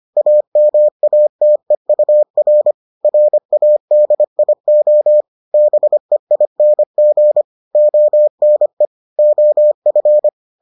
Morse code, heard as a keyed tone, about 25 wpm, 600 Hz. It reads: AMATEUR RADIO BEING ONE OF